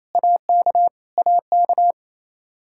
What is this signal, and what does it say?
Morse code, heard as a keyed tone, about 28 wpm, 700 Hz. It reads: AK AK
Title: K1USN SST QSO Practice 28wpm